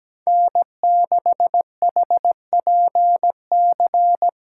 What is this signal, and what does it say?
Morse code, heard as a keyed tone, about 17 wpm, 700 Hz. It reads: N6HPC
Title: USA Call Signs - 8x Character Spacing - 17wpm